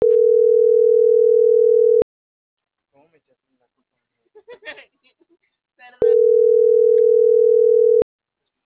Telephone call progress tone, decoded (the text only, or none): ringback tone